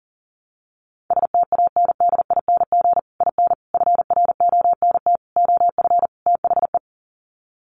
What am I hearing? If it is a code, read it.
Morse code, heard as a keyed tone, about 40 wpm, 700 Hz. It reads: STANDING IN FRONT OF THE
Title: Sets of 5 Words Encore2 40wpm